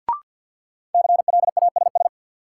Morse code, heard as a keyed tone, about 50 wpm, 700 Hz. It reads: CLASS
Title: Sentences from Top 1000 Words - Review Part 2 - New Word Drill - 50wpm